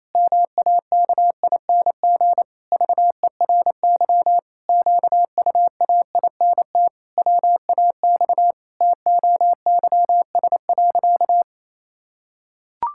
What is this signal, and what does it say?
Morse code, heard as a keyed tone, about 28 wpm, 700 Hz. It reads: MAKING VERY QUAINT WAX TOYS.
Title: Pangram Challenge v2 - 28wpm